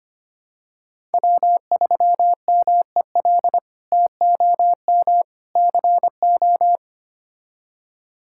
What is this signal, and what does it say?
Morse code, heard as a keyed tone, about 25 wpm, 700 Hz. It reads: W3MEL TOM CO
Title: K1USN SST QSO Practice - 25wpm